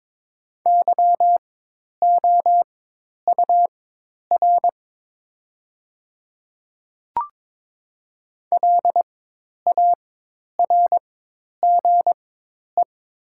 Morse code, heard as a keyed tone, about 22 wpm, spaced as for 10 wpm, 700 Hz. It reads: YOUR LARGE
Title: 4x Character Spacing - Top 300 Words - 22wpm